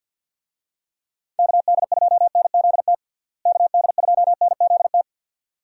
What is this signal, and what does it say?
Morse code, heard as a keyed tone, about 51 wpm, 700 Hz. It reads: KD2NZT KD2NZT
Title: USA Call Signs - 6 Characters - Repeated 2x - 50wpm